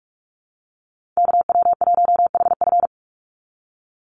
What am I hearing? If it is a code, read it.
Morse code, heard as a keyed tone, about 46 wpm, 700 Hz. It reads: KW1HF
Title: USA Call Signs 45wpm